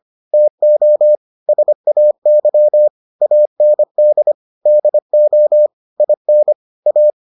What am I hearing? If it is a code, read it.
Morse code, heard as a keyed tone, about 25 wpm, 600 Hz. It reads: TO SAY AND DO IN A